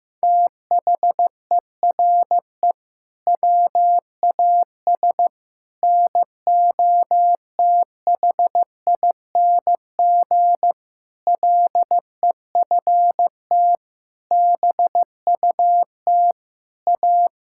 Morse code, heard as a keyed tone, about 15 wpm, 700 Hz. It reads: THERE WAS NOTHING LEFT BUT A
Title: Sentences from Top 1600 Words 15wpm